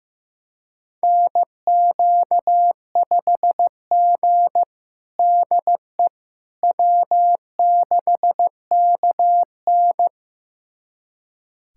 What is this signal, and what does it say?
Morse code, heard as a keyed tone, about 15 wpm, 700 Hz. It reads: NQ5G DE W6KN
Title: USA Callsign DE Callsign - 4 Characters - 15wpm